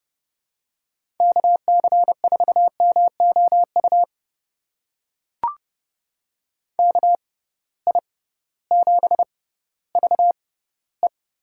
Morse code, heard as a keyed tone, about 31 wpm, spaced as for 16 wpm, 700 Hz. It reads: KC4MOU KI7VE
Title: USA Call Signs - 6 Characters - 6x Character Spacing - 30wpm